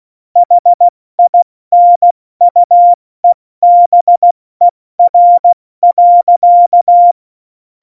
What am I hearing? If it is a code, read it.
Morse code, heard as a keyed tone, about 16 wpm, 700 Hz. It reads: HINUEBER.
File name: Morsekurs / Witze / Witz 31. Oktober 17, Tempo 70